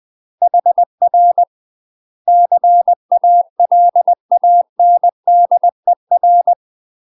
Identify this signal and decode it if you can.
Morse code, heard as a keyed tone, about 20 wpm, 700 Hz. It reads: HR CALANDER
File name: UK MASTER QSOs TRAINING/20WPM